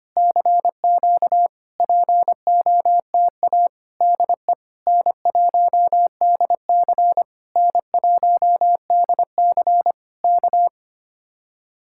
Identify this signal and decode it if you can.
Morse code, heard as a keyed tone, about 24 wpm, 700 Hz. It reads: CQ POTA DE N1DC N1DC K
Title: Speed-Racing Common QSO Phrases - 25wpm